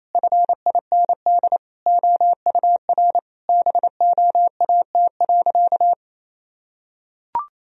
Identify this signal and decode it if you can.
Morse code, heard as a keyed tone, about 27 wpm, 700 Hz. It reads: FIND OUR BOAT.
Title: Sentences from Top 1000 Words - Review Part 4 - 28wpm